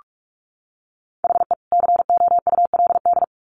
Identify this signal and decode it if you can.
Morse code, heard as a keyed tone, about 44 wpm, 700 Hz. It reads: HE COULD
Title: Sets of 3 Words 45wpm